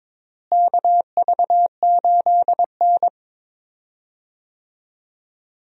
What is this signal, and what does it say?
Morse code, heard as a keyed tone, about 22 wpm, 700 Hz. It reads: KV8N
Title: USA Call Signs - 4 Characters - Repeated 3x - 22wpm